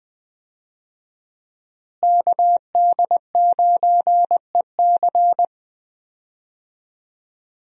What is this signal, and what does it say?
Morse code, heard as a keyed tone, about 20 wpm, 700 Hz. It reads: KD9EC
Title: USA Call Signs - 5 Characters - Repeated 2x - 20wpm